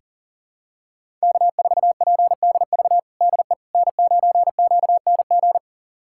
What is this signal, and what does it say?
Morse code, heard as a keyed tone, about 37 wpm, 700 Hz. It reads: K4PDV DE N9QNG
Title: USA Callsign DE Callsign 40wpm